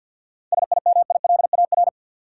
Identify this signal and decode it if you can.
Morse code, heard as a keyed tone, about 51 wpm, 700 Hz. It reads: SIMILAR
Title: Sentences from Top 1000 Words - Review Part 5 - New Word Drill - 50wpm